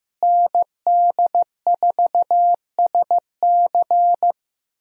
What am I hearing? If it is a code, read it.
Morse code, heard as a keyed tone, about 15 wpm, 700 Hz. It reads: ND4SC
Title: USA Call Signs - 5 Characters - Repeated 2x - 15wpm